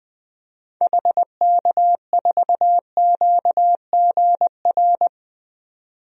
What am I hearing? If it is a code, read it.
Morse code, heard as a keyed tone, about 21 wpm, 700 Hz. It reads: HK4QGR
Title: DXCC-100 Call Signs - 6 Characters - 20wpm